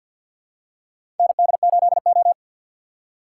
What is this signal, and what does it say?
Morse code, heard as a keyed tone, about 47 wpm, 700 Hz. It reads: ND8O
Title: USA Call Signs - 6x Character Spacing - 50wpm